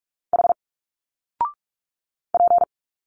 Morse code, heard as a keyed tone, about 46 wpm, 700 Hz. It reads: H P
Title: Single Letters 45wpm